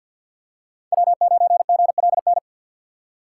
Morse code, heard as a keyed tone, about 47 wpm, 700 Hz. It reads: W9GLN
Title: USA Call Signs - 5 Characters - 2x Character Spacing - 50wpm